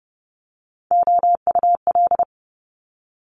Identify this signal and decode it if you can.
Morse code, heard as a keyed tone, about 30 wpm, 690 Hz. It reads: OUL
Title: Top Three Letter Combinations in Words 30wpm